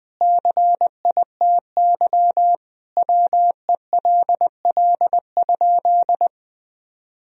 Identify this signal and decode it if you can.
Morse code, heard as a keyed tone, about 21 wpm, 700 Hz. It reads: CITY WELL?
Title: Sentences from Top 1000 Words - Review Part 2 - 20wpm